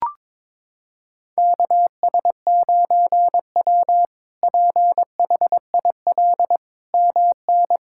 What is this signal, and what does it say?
Morse code, heard as a keyed tone, about 22 wpm, 700 Hz. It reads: KS9W PHIL MN
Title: K1USN SST QSO Practice - 22wpm